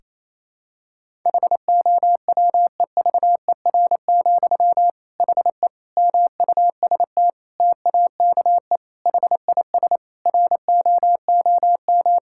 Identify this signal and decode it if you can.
Morse code, heard as a keyed tone, about 28 wpm, 700 Hz. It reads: HOWEVER, HE MUST TAKE HIS ROOM
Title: Sentences from Top 1600 Words 28wpm